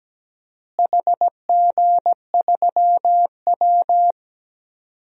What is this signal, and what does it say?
Morse code, heard as a keyed tone, about 17 wpm, 700 Hz. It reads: HG3W
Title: DXCC-100 Call Signs - 4 Characters - Repeated 3x - 17wpm